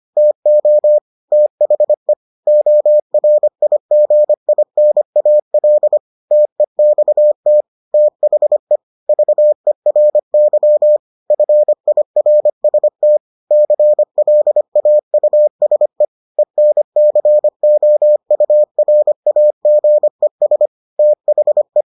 Morse code, heard as a keyed tone, about 25 wpm, 600 Hz. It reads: TO THE ORIGINAL TEXT THE VERY FIRST CLAUSE ENCOURAGES THE